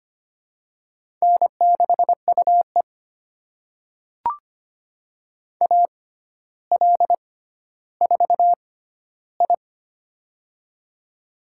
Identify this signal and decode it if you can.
Morse code, heard as a keyed tone, about 25 wpm, spaced as for 13 wpm, 700 Hz. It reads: N6UE AL4I
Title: USA Call Signs - 4 Characters - 6x Character Spacing - 25wpm